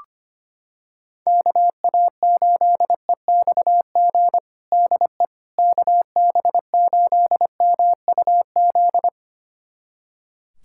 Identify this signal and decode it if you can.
Morse code, heard as a keyed tone, about 25 wpm, 700 Hz. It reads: KA8EXG DE KB8MUZ
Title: USA Callsign DE Callsign 25wpm